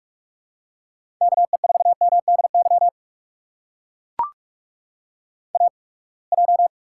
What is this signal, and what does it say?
Morse code, heard as a keyed tone, about 44 wpm, spaced as for 31 wpm, 700 Hz. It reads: KE4MDY AJ
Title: USA Call Signs - 8x Character Spacing - 45wpm